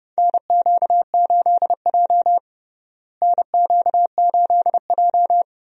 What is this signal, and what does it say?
Morse code, heard as a keyed tone, about 29 wpm, 700 Hz. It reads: NQ8J NQ8J
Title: USA Call Signs - 4 Characters - Repeated 2x - 30wpm